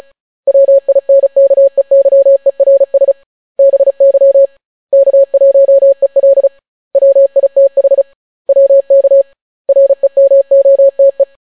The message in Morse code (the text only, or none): WINKEYERS BY K1EL WITH WK REMOTE